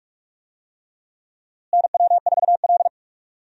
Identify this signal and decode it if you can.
Morse code, heard as a keyed tone, about 47 wpm, 700 Hz. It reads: NW4L
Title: USA Call Signs - 4 Characters - Repeated 3x - 45wpm